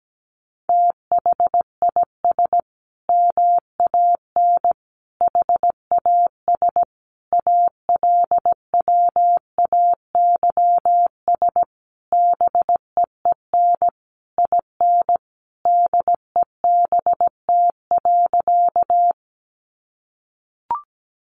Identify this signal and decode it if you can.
Morse code, heard as a keyed tone, about 17 wpm, 700 Hz. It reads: THIS MAN HAS ALWAYS BEEN IN DEBT.
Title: Sentences from Top 1300 Words 17wpm